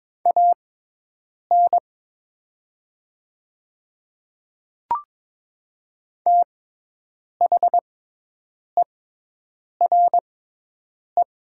Morse code, heard as a keyed tone, about 22 wpm, spaced as for 8 wpm, 700 Hz. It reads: AN THERE
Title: 6x Character Spacing - Top 300 Words - 22wpm